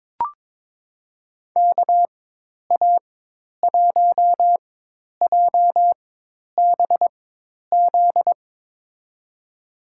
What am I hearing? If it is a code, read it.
Morse code, heard as a keyed tone, about 22 wpm, spaced as for 10 wpm, 700 Hz. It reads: KA1JBZ
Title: USA Call Signs - 6 Characters - 4x Character Spacing - 22wpm